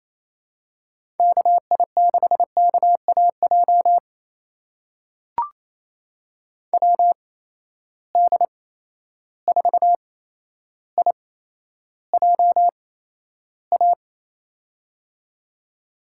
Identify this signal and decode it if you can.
Morse code, heard as a keyed tone, about 28 wpm, spaced as for 12 wpm, 700 Hz. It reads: KI6KAJ WD4IJA
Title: USA Call Signs - 6 Characters - 8x Character Spacing - 28wpm